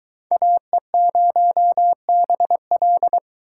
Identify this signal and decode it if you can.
Morse code, heard as a keyed tone, about 20 wpm, 700 Hz. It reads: AE0BL